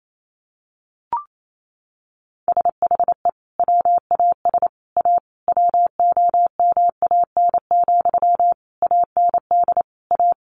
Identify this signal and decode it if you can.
Morse code, heard as a keyed tone, about 28 wpm, 700 Hz. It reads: SHE WAS A WOMAN, AND A
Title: Sentences from Top 1400 Words 28wpm